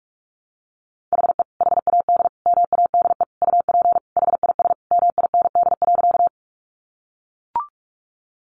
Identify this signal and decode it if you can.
Morse code, heard as a keyed tone, about 44 wpm, 700 Hz. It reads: HE HAD MADE UP HIS MIND.
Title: Sentences from Top 500 Words 45wpm